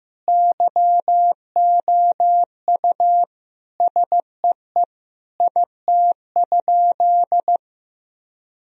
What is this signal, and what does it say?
Morse code, heard as a keyed tone, about 15 wpm, 700 Hz. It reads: YOU SEE IT?
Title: Sentences from Top 1000 Words - Review Part 4 - 15wpm